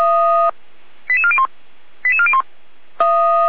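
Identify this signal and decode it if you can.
Five-tone selective call (ZVEI), heard as quick runs of five tones, tones of about 80 ms. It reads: 88401 88401
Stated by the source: Five-tone ZVEI selective call 88261 received by SDR.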